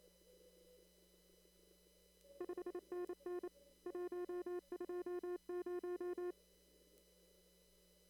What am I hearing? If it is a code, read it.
Morse code, heard as a keyed tone, about 28 wpm, 360 Hz. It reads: 5NN 120
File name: CW / kt5j wpx cw-op-k5pi